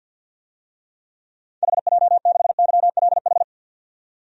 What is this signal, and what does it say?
Morse code, heard as a keyed tone, about 45 wpm, 700 Hz. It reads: HJ6YLH
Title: DXCC-100 Call Signs - 6 Characters - 50wpm